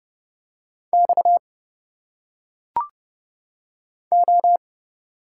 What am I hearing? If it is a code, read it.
Morse code, heard as a keyed tone, about 29 wpm, 700 Hz. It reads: X O
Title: Single Letters - Phonetic Pronunciation 30wpm